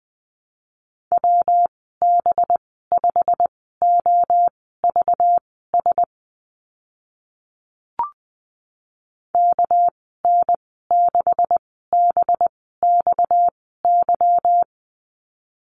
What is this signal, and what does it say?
Morse code, heard as a keyed tone, about 20 wpm, spaced as for 14 wpm, 700 Hz. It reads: WB5OVS KN6BXY
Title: 245 Lesson 39g - Letter z - Call Signs 20wpm